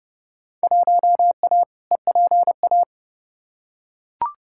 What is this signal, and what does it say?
Morse code, heard as a keyed tone, about 30 wpm, 700 Hz. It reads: 1A EPA
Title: Field Day Exchanges - 30wpm